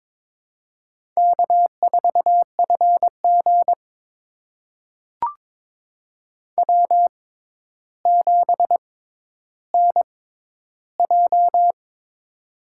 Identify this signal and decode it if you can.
Morse code, heard as a keyed tone, about 22 wpm, spaced as for 11 wpm, 700 Hz. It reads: K4FG W7NJ
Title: USA Call Signs - 4 Characters - 6x Character Spacing - 22wpm